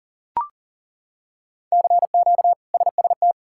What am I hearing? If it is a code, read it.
Morse code, heard as a keyed tone, about 40 wpm, 700 Hz. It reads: CQ SST
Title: K1USN SST QSO Practice - 40wpm